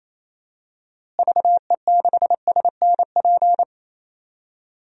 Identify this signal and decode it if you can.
Morse code, heard as a keyed tone, about 28 wpm, 700 Hz. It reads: VE6SNP
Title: DXCC-100 Call Signs - 6 Characters - Repeated 2x - 28wpm